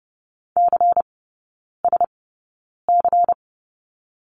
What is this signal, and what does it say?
Morse code, heard as a keyed tone, about 30 wpm, spaced as for 9 wpm, 700 Hz. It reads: CSC
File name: Lesson 16e - Letter c - Three Characters 30wpm